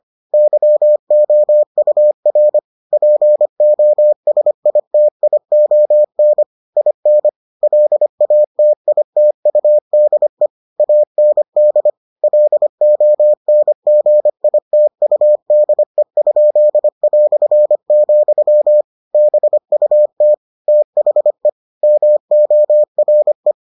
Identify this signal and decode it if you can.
Morse code, heard as a keyed tone, about 25 wpm, 600 Hz. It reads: YOUR POSITION IN LATITUDE AND LONGITUDE?", BUT THE MORE